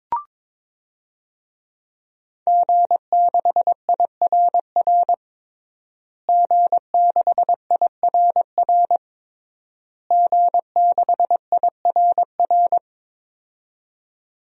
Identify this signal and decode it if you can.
Morse code, heard as a keyed tone, about 22 wpm, 700 Hz. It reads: G6IRR G6IRR G6IRR
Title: DXCC-100 Call Signs - 5 Characters - Repeated 3x - 22wpm